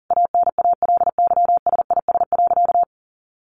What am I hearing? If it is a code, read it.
Morse code, heard as a keyed tone, about 40 wpm, 700 Hz. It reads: ANALYSIS.
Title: Sentences from Top 1400 Words 40wpm